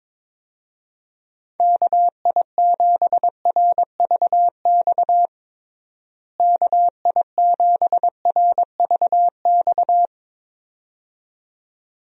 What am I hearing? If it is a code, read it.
Morse code, heard as a keyed tone, about 22 wpm, 700 Hz. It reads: KI7RVX KI7RVX
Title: USA Call Signs - 6 Characters - Repeated 2x - 22wpm